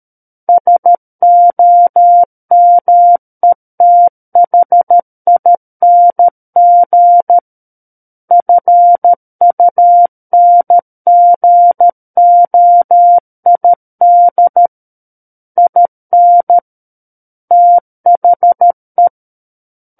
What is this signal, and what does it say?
Morse code, heard as a keyed tone, about 13 wpm, 700 Hz. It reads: SOMETHING FUNGOID IN THE
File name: Book One / 04-Chapter 4 - 13 WPM